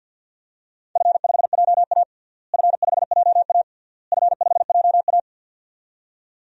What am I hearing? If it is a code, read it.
Morse code, heard as a keyed tone, about 52 wpm, 700 Hz. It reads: F5JA F5JA F5JA
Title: DXCC-100 Call Signs - 4 Characters - Repeated 3x - 50wpm